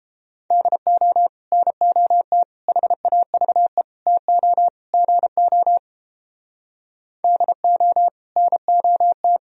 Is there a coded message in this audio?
Yes: Morse code from 0.5 to 9.4 s, heard as a keyed tone, about 29 wpm, 700 Hz. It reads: DO NOT HAVE TO GO DO NOT